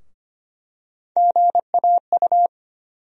Morse code, heard as a keyed tone, about 25 wpm, 700 Hz. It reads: GAU
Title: USA Call Sign Postfixes - 3 Letters 25wpm